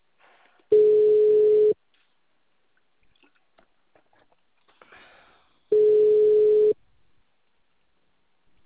Ringback tone (Europe, 425 Hz).
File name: out- -2007- - - .3354